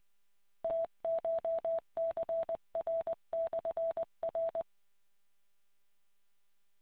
Morse code, heard as a keyed tone, about 20 wpm, 660 Hz. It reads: TCHCR/R